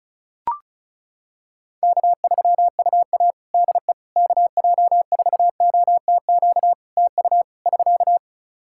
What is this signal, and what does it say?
Morse code, heard as a keyed tone, about 34 wpm, 700 Hz. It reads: K3UA DE KJ4OTQ TU <SK>
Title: Practice QSO - Exchange 4 - Ending - 35wpm